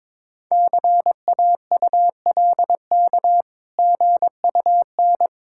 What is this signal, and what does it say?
Morse code, heard as a keyed tone, about 22 wpm, 690 Hz. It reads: CAULK GUN
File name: Difficult Words with Hints - 22wpm